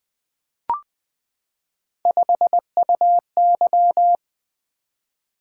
Morse code, heard as a keyed tone, about 20 wpm, 700 Hz. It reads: 5UY
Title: Three Letter-Number Combinations - 20wpm